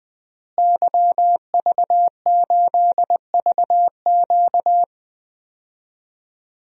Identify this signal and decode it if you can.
Morse code, heard as a keyed tone, about 21 wpm, 700 Hz. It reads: YV8VQ
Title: DXCC-100 Call Signs - 5 Characters - Repeated 2x - 20wpm